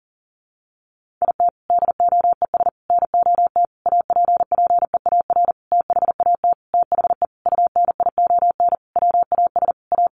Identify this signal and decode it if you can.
Morse code, heard as a keyed tone, about 40 wpm, 700 Hz. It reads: IT DOES NOT APPEAR THAT THE UNION WAS A
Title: Sentences from Top 1400 Words 40wpm